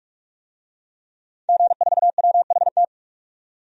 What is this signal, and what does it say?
Morse code, heard as a keyed tone, about 46 wpm, 700 Hz. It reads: G4WHT
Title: DXCC-100 Call Signs - 5 Characters - 45wpm